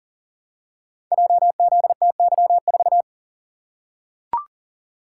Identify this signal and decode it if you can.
Morse code, heard as a keyed tone, about 39 wpm, 700 Hz. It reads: JZTY4
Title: Code Groups - 5 Letter:Numbers - 40wpm